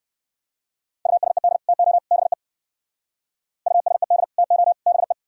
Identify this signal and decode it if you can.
Morse code, heard as a keyed tone, about 62 wpm, 700 Hz. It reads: USED TO BE USED TO BE